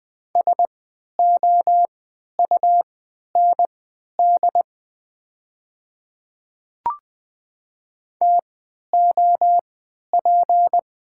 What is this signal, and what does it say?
Morse code, heard as a keyed tone, about 20 wpm, spaced as for 11 wpm, 690 Hz. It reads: SOUND TOP
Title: 3x Character Spacing - Top 500 Words - 20wpm